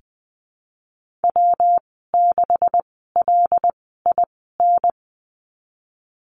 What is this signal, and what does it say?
Morse code, heard as a keyed tone, about 20 wpm, spaced as for 15 wpm, 700 Hz. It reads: W6LIN
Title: 147 Lesson 24g - Letter y - Call Signs 20wpm